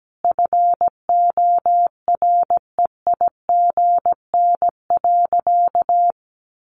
Morse code, heard as a keyed tone, about 17 wpm, 700 Hz. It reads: FOREIGN.
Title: Sentences from Top 1400 Words 17wpm